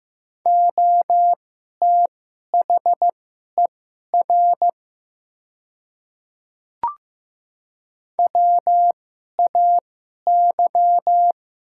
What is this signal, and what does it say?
Morse code, heard as a keyed tone, about 15 wpm, spaced as for 11 wpm, 700 Hz. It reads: OTHER WAY
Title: 2x Character Spacing - Top 100 Words - 15wpm